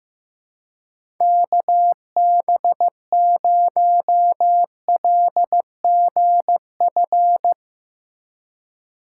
Morse code, heard as a keyed tone, about 15 wpm, 700 Hz. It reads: KB0LGF
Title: K1USN SST QSO Practice - 15wpm